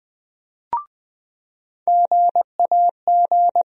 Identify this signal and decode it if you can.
Morse code, heard as a keyed tone, about 20 wpm, 700 Hz. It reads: GAG
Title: USA Call Sign Postfixes - 3 Letters 20wpm